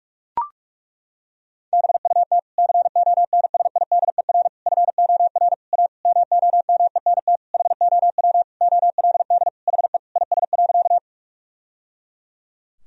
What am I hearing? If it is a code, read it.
Morse code, heard as a keyed tone, about 41 wpm, 700 Hz. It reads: BUT CONSIDER FOR A MOMENT HOW OLD HE IS.
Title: Sentences from Top 1000 Words - Review Part 4 - 45wpm